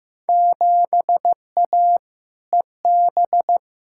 Morse code, heard as a keyed tone, about 15 wpm, 700 Hz. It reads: 7A EB
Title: Field Day Exchanges Encore - 15wpm